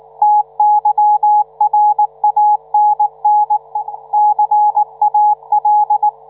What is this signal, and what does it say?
Morse code, heard as a keyed tone, about 19 wpm, 840 Hz, with radio noise. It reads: TYRANN CAL